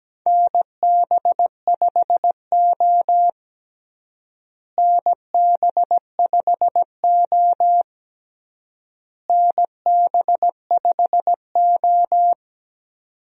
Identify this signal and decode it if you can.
Morse code, heard as a keyed tone, about 17 wpm, 700 Hz. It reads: NB5O NB5O NB5O
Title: USA Call Signs - 4 Characters - Repeated 3x - 17wpm